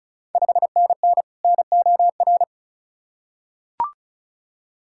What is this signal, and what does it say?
Morse code, heard as a keyed tone, about 36 wpm, 690 Hz. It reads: 5NN NOR
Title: Ontario QSO Party Exchange 35wpm